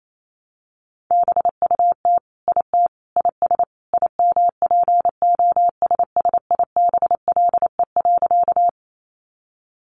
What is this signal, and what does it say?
Morse code, heard as a keyed tone, about 28 wpm, 700 Hz. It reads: BUT IT IS IMPOSSIBLE.
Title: Sentences from Top 1100 Words 28wpm